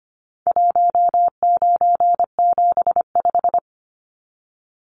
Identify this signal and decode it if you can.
Morse code, heard as a keyed tone, about 25 wpm, 700 Hz. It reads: 1975
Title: Numbers - 4-Digit 25wpm